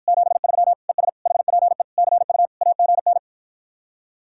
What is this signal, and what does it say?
Morse code, heard as a keyed tone, about 54 wpm, 700 Hz. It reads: 73 ES HPE CU AGN